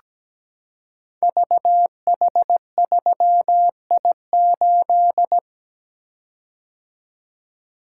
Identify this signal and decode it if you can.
Morse code, heard as a keyed tone, about 17 wpm, 700 Hz. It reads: VH3I8
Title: Code Groups - 5 Letter:Numbers - 17wpm